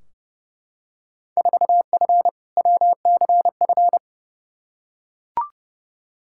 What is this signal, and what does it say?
Morse code, heard as a keyed tone, about 30 wpm, 700 Hz. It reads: 4F WCF
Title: Field Day Exchanges - 30wpm